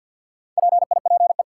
Morse code, heard as a keyed tone, about 49 wpm, 700 Hz. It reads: PIPE
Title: Sentences from Top 1500 Words - New Word Drill Alt - 50wpm